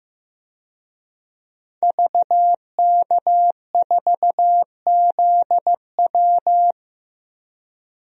Morse code, heard as a keyed tone, about 15 wpm, 700 Hz. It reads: VK4ZW
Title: DXCC-100 Call Signs - 5 Characters - 15wpm